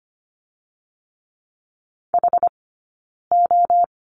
Morse code, heard as a keyed tone, about 25 wpm, 700 Hz. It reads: H O
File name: course / 25:10wpm / 092 Lesson 16d - Letter c - Two Characters 25wpm